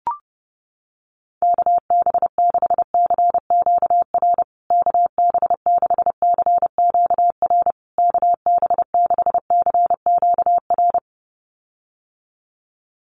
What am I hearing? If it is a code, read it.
Morse code, heard as a keyed tone, about 30 wpm, 700 Hz. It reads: KB6CQR KB6CQR KB6CQR
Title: USA Call Signs Repeated 3x 30wpm